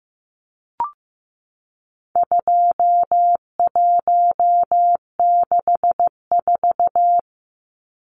Morse code, heard as a keyed tone, about 15 wpm, 700 Hz. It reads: 2164
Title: Numbers - 4-Digit 15wpm